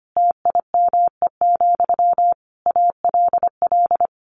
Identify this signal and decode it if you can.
Morse code, heard as a keyed tone, about 25 wpm, 700 Hz. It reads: TIME, ALL
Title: A Sound Of Thunder 25wpm